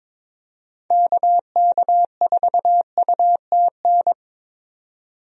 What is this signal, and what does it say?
Morse code, heard as a keyed tone, about 22 wpm, 700 Hz. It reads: KK4UTN
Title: USA Call Signs - 6 Characters - 22wpm